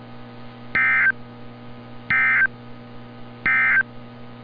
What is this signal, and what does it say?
Emergency Alert System (SAME) tones, heard as harsh buzzing bursts. It is heard from 0.6 to 3.9 s.
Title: NOAA Weather Radio - Altoona, PA